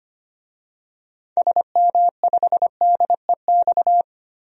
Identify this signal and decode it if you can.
Morse code, heard as a keyed tone, about 25 wpm, 700 Hz. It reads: SM5DEX
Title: DXCC-100 Call Signs - 6 Characters - 25wpm